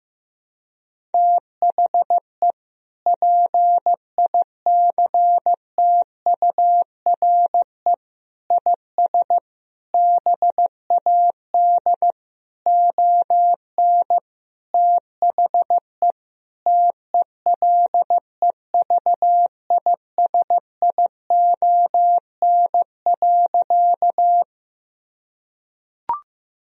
Morse code, heard as a keyed tone, about 15 wpm, 700 Hz. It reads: THE PICTURE IS BAD ON THE TELEVISION.
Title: Sentences from Top 1000 Words - Review Part 4 - 15wpm